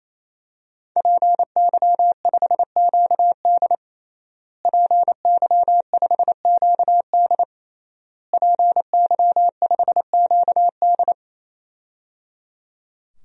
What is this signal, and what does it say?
Morse code, heard as a keyed tone, about 28 wpm, 700 Hz. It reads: PY5QD PY5QD PY5QD
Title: DXCC-100 Call Signs - 5 Characters - Repeated 3x - 28wpm